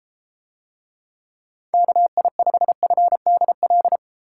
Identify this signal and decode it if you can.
Morse code, heard as a keyed tone, about 29 wpm, 700 Hz. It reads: KI5FDL